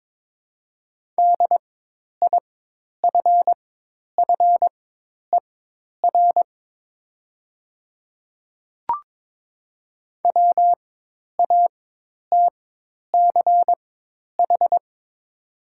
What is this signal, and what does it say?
Morse code, heard as a keyed tone, about 22 wpm, spaced as for 10 wpm, 700 Hz. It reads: DIFFER WATCH
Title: 4x Character Spacing - Top 300 Words - 22wpm